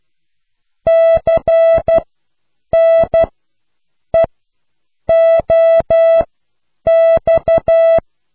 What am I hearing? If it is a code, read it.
Morse code, heard as a keyed tone, about 12 wpm, spaced as for 7 wpm, 660 Hz. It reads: CNEOX